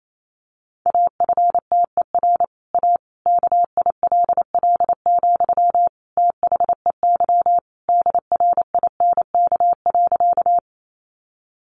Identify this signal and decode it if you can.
Morse code, heard as a keyed tone, about 28 wpm, 700 Hz. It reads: AFTER A KILL, THEY DRINK.
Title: Sentences from Top 1500 Words 28wpm